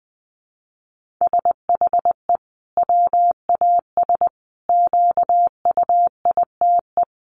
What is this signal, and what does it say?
Morse code, heard as a keyed tone, about 20 wpm, 700 Hz. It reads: SHE WAS QUITE
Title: Sentences from Top 1500 Words 20wpm